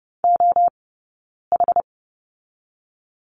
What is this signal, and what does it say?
Morse code, heard as a keyed tone, about 30 wpm, 700 Hz. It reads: O H
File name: 092 Lesson 16d - Letter c - Two Characters 30wpm